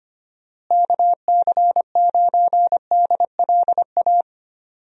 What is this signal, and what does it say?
Morse code, heard as a keyed tone, about 25 wpm, 700 Hz. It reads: KC9DLA
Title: USA Call Signs - 6 Characters - Repeated 3x - 25wpm